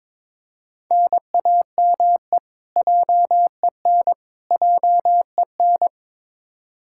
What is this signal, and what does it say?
Morse code, heard as a keyed tone, about 22 wpm, 700 Hz. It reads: NAME JEN JEN
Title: K1USN SST QSO Practice 22wpm